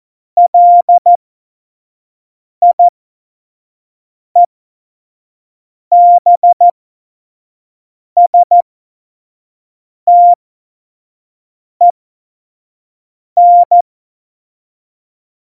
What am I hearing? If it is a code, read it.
Morse code, heard as a keyed tone, about 14 wpm, spaced as for 5 wpm, 700 Hz. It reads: LIEBSTEN
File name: Morsekurs / Witze / Witz 06 March 2018, 25 F70